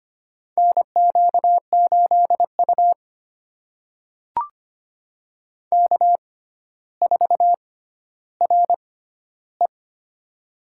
Morse code, heard as a keyed tone, about 25 wpm, spaced as for 13 wpm, 700 Hz. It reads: NQ8U K4RE
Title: USA Call Signs - 4 Characters - 6x Character Spacing - 25wpm